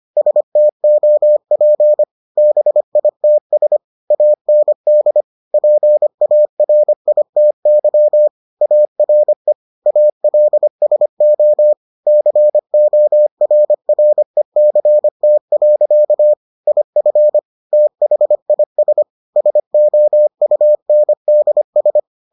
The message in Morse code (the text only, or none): STOP BITS AND PARITY ARE ALSO CORRECT. IF THIS SOUNDS